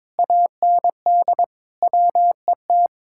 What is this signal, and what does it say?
Morse code, heard as a keyed tone, about 22 wpm, 700 Hz. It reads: AND WET
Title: Speed-Racing Common QSO Phrases - 22wpm